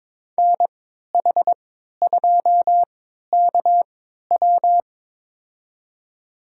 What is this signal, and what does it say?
Morse code, heard as a keyed tone, about 21 wpm, spaced as for 12 wpm, 700 Hz. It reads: NH2KW
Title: USA Call Signs - 5 Characters - 3x Character Spacing - 22wpm